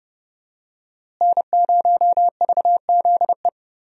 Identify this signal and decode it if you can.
Morse code, heard as a keyed tone, about 30 wpm, 700 Hz. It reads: N0VZE
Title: USA Call Signs - 5 Characters - Repeated 3x - 30wpm